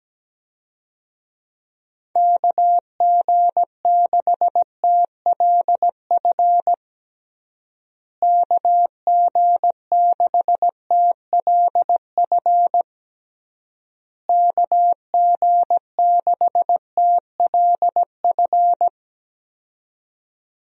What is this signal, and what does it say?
Morse code, heard as a keyed tone, about 17 wpm, 700 Hz. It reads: KG6TLF KG6TLF KG6TLF
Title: USA Call Signs - 6 Characters - Repeated 3x - 17wpm